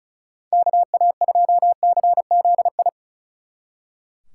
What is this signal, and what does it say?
Morse code, heard as a keyed tone, about 44 wpm, 700 Hz. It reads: KA2CZI